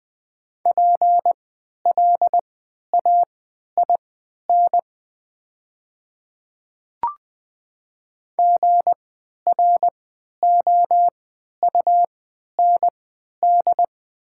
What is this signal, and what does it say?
Morse code, heard as a keyed tone, about 20 wpm, spaced as for 11 wpm, 700 Hz. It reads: PLAIN GROUND
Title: 3x Character Spacing - Top 500 Words - 20wpm